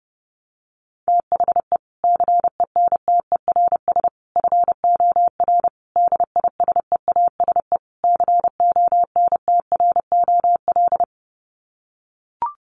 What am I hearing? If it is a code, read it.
Morse code, heard as a keyed tone, about 30 wpm, 700 Hz. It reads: THE CENTERS FOR DISEASE CONTROL
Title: Sets of 5 Words Encore2 30wpm